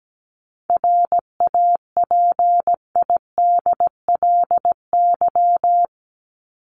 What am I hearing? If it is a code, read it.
Morse code, heard as a keyed tone, about 17 wpm, 700 Hz. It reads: RAPIDLY
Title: LY Suffix - Top 100 Words with Base Words 17wpm